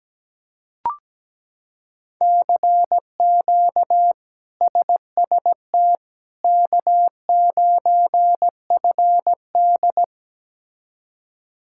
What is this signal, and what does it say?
Morse code, heard as a keyed tone, about 17 wpm, 700 Hz. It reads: CQ SST K9FD
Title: K1USN SST QSO Practice - 17wpm